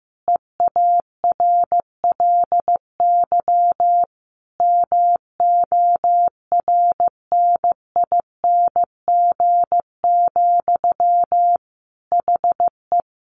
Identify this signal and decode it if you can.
Morse code, heard as a keyed tone, about 15 wpm, 700 Hz. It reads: EARLY MORNING, HE
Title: Sentences from Top 1400 Words 15wpm